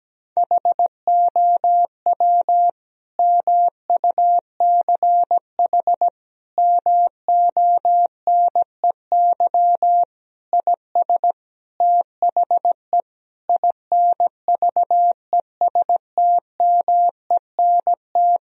Morse code, heard as a keyed tone, about 17 wpm, 700 Hz. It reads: HOW MUCH MONEY IS THE INVESTMENT
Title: Sentences from Top 1000 Words - Review Part 4 - 17wpm